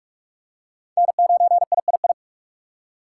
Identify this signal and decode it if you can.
Morse code, heard as a keyed tone, about 46 wpm, 700 Hz. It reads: N9III
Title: USA Call Signs - 5 Characters - 4x Character Spacing - 45wpm